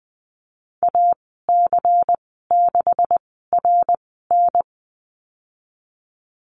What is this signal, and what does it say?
Morse code, heard as a keyed tone, about 19 wpm, spaced as for 14 wpm, 700 Hz. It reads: AC6RN
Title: 173 Lesson 28g - Number 9 - Call Signs 20wpm